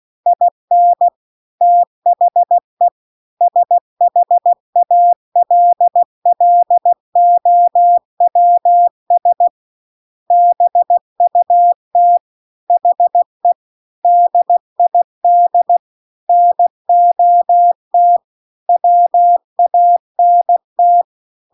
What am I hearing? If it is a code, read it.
Morse code, heard as a keyed tone, about 16 wpm, 700 Hz. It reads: IN THE SHALLOWS BUT HE DID NOT WANT
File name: CW-The-Big-Twoheaded-River-2-16WPM